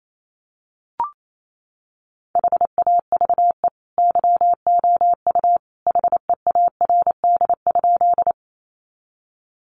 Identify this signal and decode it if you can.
Morse code, heard as a keyed tone, about 28 wpm, 700 Hz. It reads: HAVE YOU HEARD?
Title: Sentences from Top 400 Words 28wpm